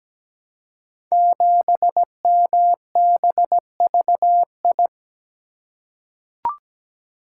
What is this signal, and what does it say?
Morse code, heard as a keyed tone, about 17 wpm, 690 Hz. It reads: 7MBVI
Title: Code Groups - 5 Letter:Numbers - 17wpm